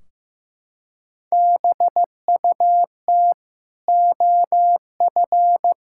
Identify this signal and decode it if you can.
Morse code, heard as a keyed tone, about 16 wpm, 700 Hz. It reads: BUT OF
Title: Sentences from Top 1600 Words 15wpm